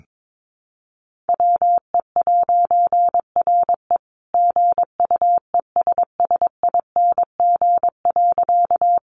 Morse code, heard as a keyed tone, about 22 wpm, 700 Hz. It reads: WE'RE GUESSING.
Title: A Sound Of Thunder 22wpm